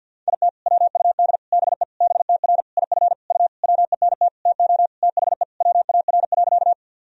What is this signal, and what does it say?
Morse code, heard as a keyed tone, about 52 wpm, 700 Hz. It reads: IT WUD BE BTR IF U WENT TO THE WAR.
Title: Sentences from Top 400 Words and Abbreviations 50wpm